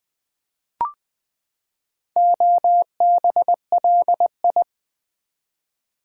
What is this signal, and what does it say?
Morse code, heard as a keyed tone, about 20 wpm, 700 Hz. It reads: OBLI
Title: Build A Word - 4 Letters At A Time - 10-Letter Words - 20wpm